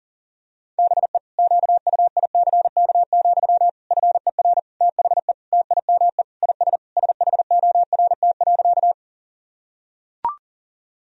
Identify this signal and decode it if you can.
Morse code, heard as a keyed tone, about 37 wpm, 700 Hz. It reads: BE QUICK, FER THE TIME IS SHORT.
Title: Sentences from Top 500 Words and Abbreviations 40wpm